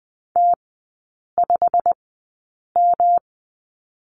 Morse code, heard as a keyed tone, about 20 wpm, spaced as for 8 wpm, 700 Hz. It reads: T5M
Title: 191 Lesson 31e - Letter v - Three Characters 20wpm